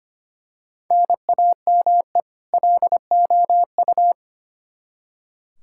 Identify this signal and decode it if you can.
Morse code, heard as a keyed tone, about 25 wpm, 700 Hz. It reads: NAME LOU
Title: Practice QSO - Exchange 1 - Beginning - 25wpm